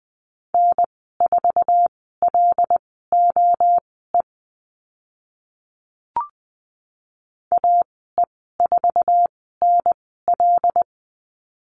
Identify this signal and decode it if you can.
Morse code, heard as a keyed tone, about 20 wpm, spaced as for 14 wpm, 700 Hz. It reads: N4LOE AE4NL
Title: 082 Lesson 14g - Number 2 - Call Signs 20wpm